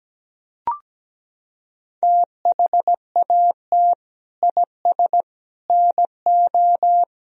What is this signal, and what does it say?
Morse code, heard as a keyed tone, about 17 wpm, 700 Hz. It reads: THAT IS NO
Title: Sentences from Top 1000 Words - Review Part 3 - 17wpm